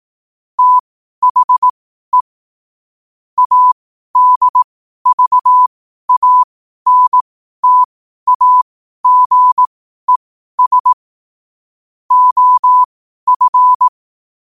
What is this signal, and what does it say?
Morse code, heard as a keyed tone, about 18 wpm, spaced as for 12 wpm, 1000 Hz. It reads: THE ADVANTAGES OF
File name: qotdfiles / 12wpm